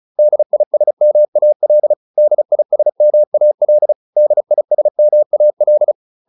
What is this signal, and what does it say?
Morse code, heard as a keyed tone, about 34 wpm, 600 Hz. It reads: DISMAL DISMAL DISMAL
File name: DIS6 35